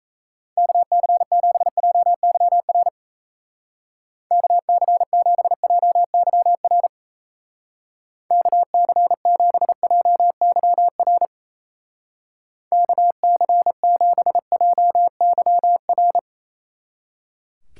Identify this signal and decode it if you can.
Morse code, heard as a keyed tone, about 35 wpm, 700 Hz. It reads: KC7JYR KC7JYR KC7JYR KC7JYR
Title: Speed Racing - USA 2x3 Call Signs 28wpm